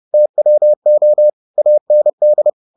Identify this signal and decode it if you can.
Morse code, heard as a keyed tone, about 30 wpm, 600 Hz. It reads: TWO AND